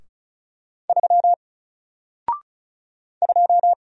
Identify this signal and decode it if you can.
Morse code, heard as a keyed tone, about 35 wpm, 700 Hz. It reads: 3 2
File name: Single-Letter-Number-35wpm